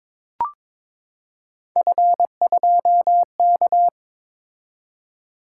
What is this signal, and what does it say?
Morse code, heard as a keyed tone, about 22 wpm, 700 Hz. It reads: F2K
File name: practice / 22wpm / Three Letter-Number Combinations - 22wpm